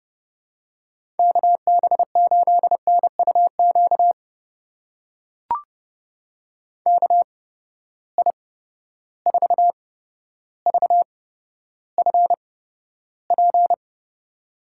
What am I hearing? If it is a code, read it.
Morse code, heard as a keyed tone, about 30 wpm, spaced as for 13 wpm, 700 Hz. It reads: KB8NUQ KI4VFP
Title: USA Call Signs - 6 Characters - 8x Character Spacing - 30wpm